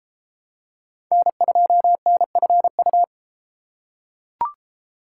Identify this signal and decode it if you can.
Morse code, heard as a keyed tone, about 29 wpm, 700 Hz. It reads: N2NFU